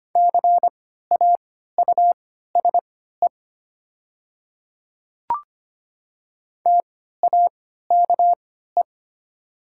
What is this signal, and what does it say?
Morse code, heard as a keyed tone, about 25 wpm, spaced as for 14 wpm, 690 Hz. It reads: CAUSE TAKE
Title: 3x Character Spacing - Top 200 Words - 25wpm